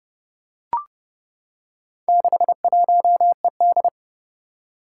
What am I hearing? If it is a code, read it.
Morse code, heard as a keyed tone, about 28 wpm, 700 Hz. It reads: 61ED
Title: Four Challenging Letter-Number Combinations 30wpm